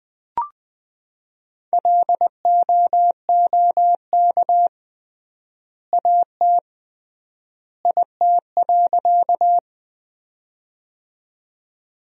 Morse code, heard as a keyed tone, about 20 wpm, 700 Hz. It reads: LOOK AT IT.
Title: 3x Word-Spacing - Sentences from Top 100 Words - 20wpm